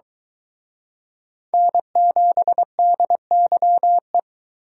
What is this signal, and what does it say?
Morse code, heard as a keyed tone, about 23 wpm, 700 Hz. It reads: N7DYE